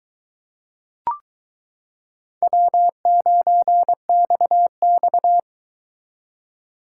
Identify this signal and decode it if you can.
Morse code, heard as a keyed tone, about 22 wpm, 700 Hz. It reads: W9XX